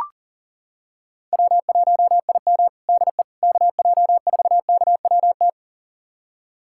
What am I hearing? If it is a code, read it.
Morse code, heard as a keyed tone, about 40 wpm, 700 Hz. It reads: W1IM DE KJ4KWT
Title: USA Callsign DE Callsign 40wpm